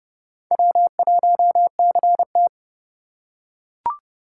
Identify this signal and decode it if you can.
Morse code, heard as a keyed tone, about 30 wpm, 700 Hz. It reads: W1CT